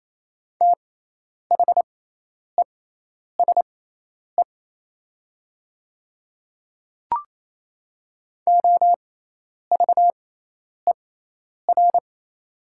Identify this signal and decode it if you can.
Morse code, heard as a keyed tone, about 28 wpm, spaced as for 10 wpm, 700 Hz. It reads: THESE OVER
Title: 6x Character Spacing - Top 200 Words - 28wpm